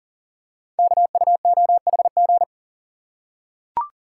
Morse code, heard as a keyed tone, about 39 wpm, 700 Hz. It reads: KUOHG
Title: Code Groups - 5 Letters - 40wpm